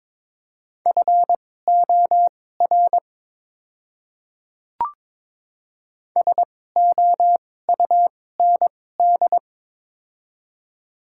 Morse code, heard as a keyed tone, about 21 wpm, spaced as for 16 wpm, 700 Hz. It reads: FOR SOUND
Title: 2x Character Spacing - Top 300 Words - 22wpm